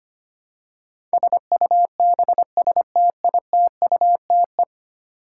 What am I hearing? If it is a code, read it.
Morse code, heard as a keyed tone, about 25 wpm, 700 Hz. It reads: SUBSTITUTE
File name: 10 Letter Words - First 3 Letters Given - 25wpm